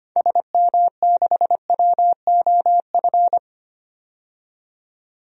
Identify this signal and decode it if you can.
Morse code, heard as a keyed tone, about 25 wpm, 700 Hz. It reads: SM6WOF
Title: DXCC-100 Call Signs - 6 Characters - 25wpm